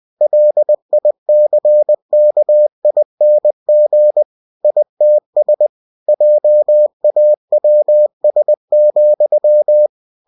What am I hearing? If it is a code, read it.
Morse code, heard as a keyed tone, about 20 wpm, 600 Hz. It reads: LICKING ITS JAWS,
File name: cw-books / winnie-the-pooh / winnie-20wpm-page-0040